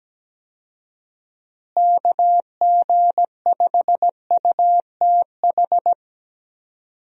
Morse code, heard as a keyed tone, about 17 wpm, 700 Hz. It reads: KG5UTH
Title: USA Call Signs - 6 Characters - Repeated 3x - 17wpm